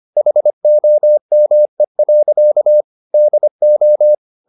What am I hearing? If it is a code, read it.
Morse code, heard as a keyed tone, about 25 wpm, 600 Hz. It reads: HOME. DO
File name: the old man and the sea part 1 25wpm